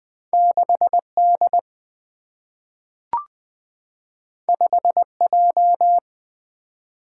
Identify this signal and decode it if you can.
Morse code, heard as a keyed tone, about 20 wpm, 700 Hz. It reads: 6D 5J
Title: Single-Letter Single-Number 20wpm